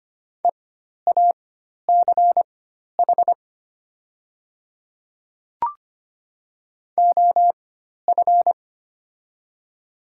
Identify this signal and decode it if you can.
Morse code, heard as a keyed tone, about 25 wpm, spaced as for 12 wpm, 700 Hz. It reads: EACH OF
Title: 4x Character Spacing - Top 100 Words - 25wpm